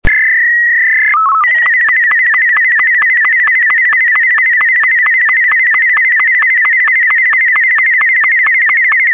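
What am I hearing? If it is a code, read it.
SSTV (slow-scan television), Martin 2.
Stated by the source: Martin M2 SSTV mode sample, 58 seconds long.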